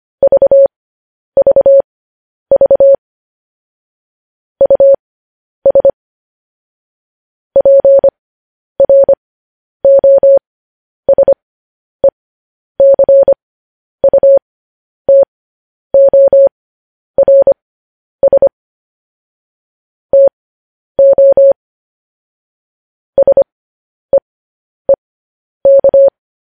Morse code, heard as a keyed tone, about 25 wpm, spaced as for 10 wpm, 550 Hz. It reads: VVV US PROSECUTORS TO SEEK
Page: news headlines in morse code at 10 wpm